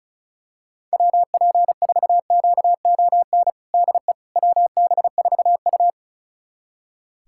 Morse code, heard as a keyed tone, about 34 wpm, 700 Hz. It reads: WP4QON DE WB4U
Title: USA Callsign DE Callsign 35wpm